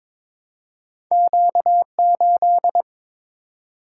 Morse code, heard as a keyed tone, about 22 wpm, 700 Hz. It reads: Q8
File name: practice / 22wpm / Two Letter-Number Combinations - 22wpm